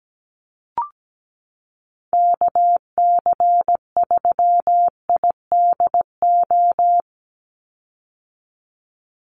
Morse code, heard as a keyed tone, about 17 wpm, 700 Hz. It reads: KC3IDO
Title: USA Call Signs 17wpm